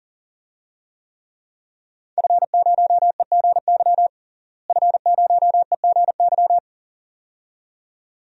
Morse code, heard as a keyed tone, about 40 wpm, 700 Hz. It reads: F0EGY F0EGY
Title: DXCC-100 Call Signs - 5 Characters - Repeated 2x - 40wpm